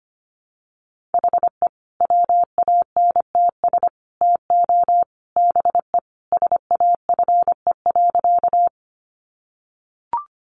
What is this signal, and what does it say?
Morse code, heard as a keyed tone, about 25 wpm, 700 Hz. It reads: HE WANTS TO BE SAFE.
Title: Additional Easy Phrases 25wpm